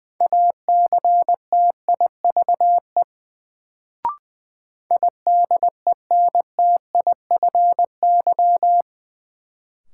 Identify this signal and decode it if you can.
Morse code, heard as a keyed tone, about 20 wpm, 700 Hz. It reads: ACTIVE IDENTIFY
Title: Sentences from Top 900 Words - New Word Drill - 20wpm